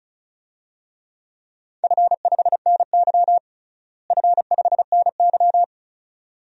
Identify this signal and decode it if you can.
Morse code, heard as a keyed tone, about 36 wpm, 700 Hz. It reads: F5NY F5NY
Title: DXCC-100 Call Signs - 4 Characters - Repeated 2x - 35wpm